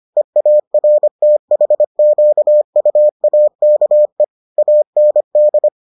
Morse code, heard as a keyed tone, about 25 wpm, 600 Hz. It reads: EARTHQUAKE AND